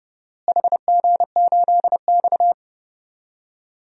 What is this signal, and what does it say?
Morse code, heard as a keyed tone, about 30 wpm, 700 Hz. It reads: HG8X
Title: DXCC-100 Call Signs - 4 Characters - Repeated 3x - 30wpm